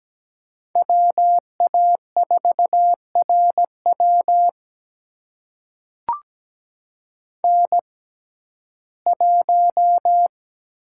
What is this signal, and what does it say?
Morse code, heard as a keyed tone, about 17 wpm, spaced as for 12 wpm, 700 Hz. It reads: WA4RW N1
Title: USA Call Signs - 5 Characters - 6x Character Spacing - 17wpm